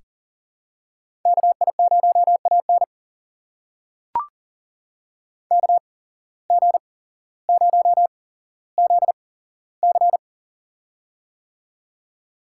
Morse code, heard as a keyed tone, about 37 wpm, spaced as for 17 wpm, 700 Hz. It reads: KI0AN KG0ZC
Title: USA Call Signs - 5 Characters - 8x Character Spacing - 40wpm